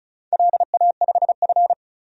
Morse code, heard as a keyed tone, about 36 wpm, 690 Hz. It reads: LA5F
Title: DXCC-100 Call Signs - 4 Characters - 35wpm